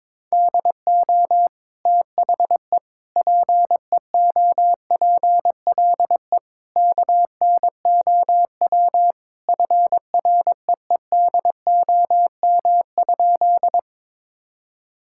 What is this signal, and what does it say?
Morse code, heard as a keyed tone, about 22 wpm, 700 Hz. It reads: DO THE PEOPLE KNOW FREEDOM?
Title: Sentences from Top 1000 Words - Review Part 5 - 22wpm